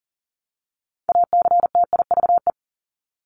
Morse code, heard as a keyed tone, about 40 wpm, 700 Hz. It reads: ACTIVE
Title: Top 751-1000 Words 40wpm